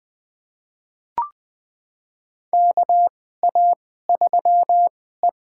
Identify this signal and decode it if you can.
Morse code, heard as a keyed tone, about 20 wpm, spaced as for 14 wpm, 700 Hz. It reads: KA3E
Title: USA Call Signs - 4 Characters - 2x Character Spacing - 20wpm